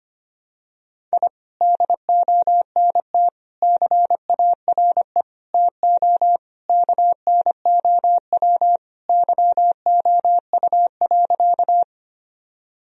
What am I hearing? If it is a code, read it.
Morse code, heard as a keyed tone, about 25 wpm, 700 Hz. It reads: I DONT CARE TO KNOW YOU.
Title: Sentences from Top 1000 Words - Review Part 1 - 25wpm